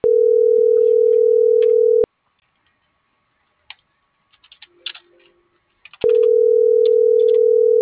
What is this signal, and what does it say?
Ringback tone (North America), 440 + 480 Hz.